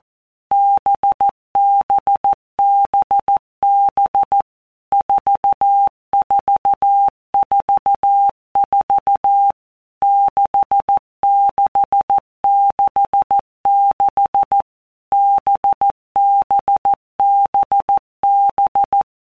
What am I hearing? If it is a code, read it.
Morse code, heard as a keyed tone, about 14 wpm, 800 Hz. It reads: BBBB 4444 6666 BBBB
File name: Basic Letter Groups By Blocks / Groups@13 / VB46@13